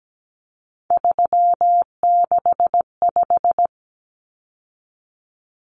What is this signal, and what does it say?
Morse code, heard as a keyed tone, about 17 wpm, 700 Hz. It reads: 365
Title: Common Cut Numbers - 3-Digit 17wpm